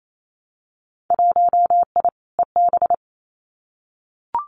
Morse code, heard as a keyed tone, about 28 wpm, 700 Hz. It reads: 1I EB
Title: Winter Field Day Exchanges Encore 28wpm